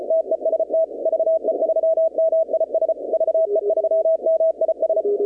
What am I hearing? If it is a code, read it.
Morse code, heard as a keyed tone, about 35 wpm, 660 Hz. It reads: TEST VE3MIS VE3MIS
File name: contests / 2019-cw-wpx-kt5j-k5pi- -2330 0-21